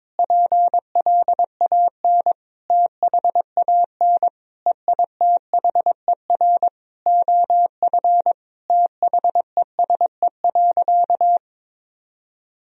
Morse code, heard as a keyed tone, about 22 wpm, 700 Hz. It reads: PLAN THAN EITHER OF THESE.
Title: Sentences from Top 1000 Words - Review Part 4 - 22wpm